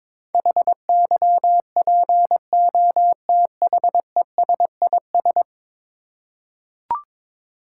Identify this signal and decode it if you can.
Morse code, heard as a keyed tone, about 22 wpm, 700 Hz. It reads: HYPOTHESIS
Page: morse-code-ninja-practice-22wpm